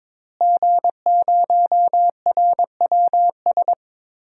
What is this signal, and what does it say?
Morse code, heard as a keyed tone, about 22 wpm, 700 Hz. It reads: G0RWS
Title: DXCC-100 Call Signs - 5 Characters - Repeated 2x - 22wpm